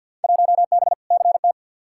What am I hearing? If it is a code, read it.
Morse code, heard as a keyed tone, about 51 wpm, 700 Hz. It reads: 1B CT
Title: Field Day Exchanges Encore - 50wpm